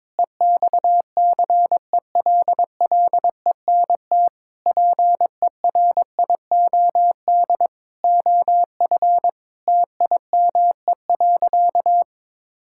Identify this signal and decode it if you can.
Morse code, heard as a keyed tone, about 22 wpm, 700 Hz. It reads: EXCELLENT PERIOD OF TIME.
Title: Sentences from Top 1000 Words - Review Part 5 - 22wpm